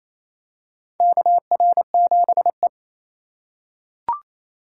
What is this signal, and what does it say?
Morse code, heard as a keyed tone, about 28 wpm, 700 Hz. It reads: KR7E
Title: USA Call Signs - 4 Characters - 8x Character Spacing - 28wpm